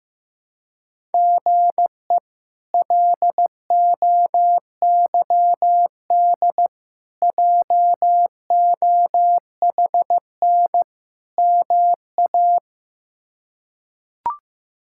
Morse code, heard as a keyed tone, about 15 wpm, 700 Hz. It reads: GE LOYD JOHN MA
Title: K1USN SST QSO Practice - 15wpm